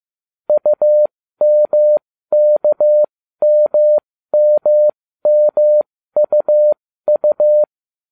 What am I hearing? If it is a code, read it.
Morse code, heard as a keyed tone, about 15 wpm, spaced as for 13 wpm, 600 Hz. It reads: UMKMMMUU